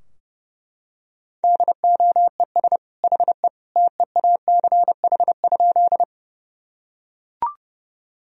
Morse code, heard as a keyed tone, about 30 wpm, 700 Hz. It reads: DOES HE TEACH?
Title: Sentences from Top 1000 Words - Review Part 2 - 30wpm